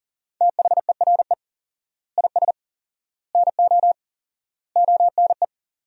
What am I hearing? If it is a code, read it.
Morse code, heard as a keyed tone, about 40 wpm, 700 Hz. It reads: THERE IS NO ONE
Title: 4x Word-Spacing - Sentences from Top 500 Words - 40wpm